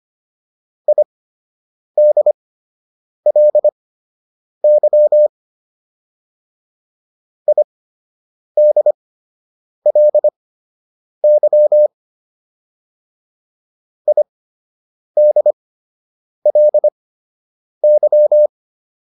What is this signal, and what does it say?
Morse code, heard as a keyed tone, about 25 wpm, spaced as for 8 wpm, 600 Hz. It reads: IDLY IDLY IDLY